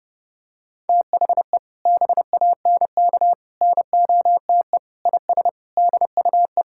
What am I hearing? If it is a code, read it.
Morse code, heard as a keyed tone, about 30 wpm, 700 Hz. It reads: THE BANK NOTE IS DUE
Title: Sentences from Top 1000 Words - Review Part 6 - 30wpm